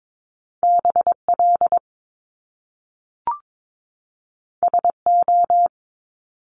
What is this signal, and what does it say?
Morse code, heard as a keyed tone, about 22 wpm, 700 Hz. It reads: BL SO
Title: Top Two Letter Combinations in Words 22wpm